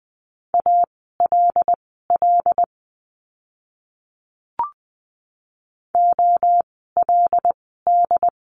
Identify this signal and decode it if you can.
Morse code, heard as a keyed tone, about 20 wpm, spaced as for 14 wpm, 700 Hz. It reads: ALL OLD
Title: 230 Lesson 37f - Letter x - Words 20wpm